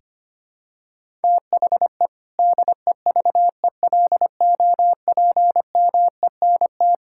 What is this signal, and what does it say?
Morse code, heard as a keyed tone, about 25 wpm, 700 Hz. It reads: THE DEVELOPMENT
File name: practice / 25wpm / Sentences with 3-Word Phrases - 25wpm